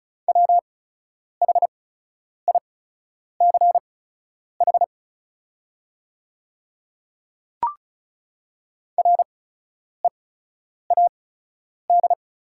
Morse code, heard as a keyed tone, about 34 wpm, spaced as for 10 wpm, 690 Hz. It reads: WHICH READ
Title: 8x Character Spacing - Top 200 Words - 35wpm